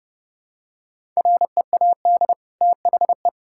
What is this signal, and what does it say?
Morse code, heard as a keyed tone, about 32 wpm, 700 Hz. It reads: READ THE
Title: Sentences from Top 1000 Words - Review Part 5 - 30wpm